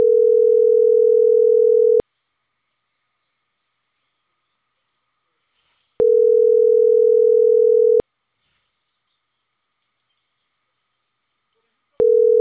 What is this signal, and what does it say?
Ringback tone (North America), 440 + 480 Hz.